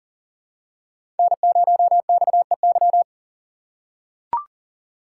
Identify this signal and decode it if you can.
Morse code, heard as a keyed tone, about 40 wpm, 700 Hz. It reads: N0XEY
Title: USA Call Signs - 5 Characters - 3x Character Spacing - 40wpm